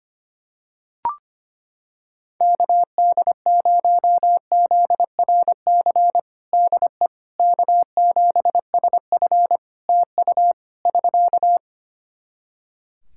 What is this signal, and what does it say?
Morse code, heard as a keyed tone, about 25 wpm, 700 Hz. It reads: KD0ZRC DE K7SF TU <SK>
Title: Practice QSO - Exchange 4 - Ending - 25wpm